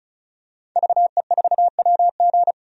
Morse code, heard as a keyed tone, about 34 wpm, 700 Hz. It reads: VE4WG
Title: DXCC-100 Call Signs - 5 Characters - 35wpm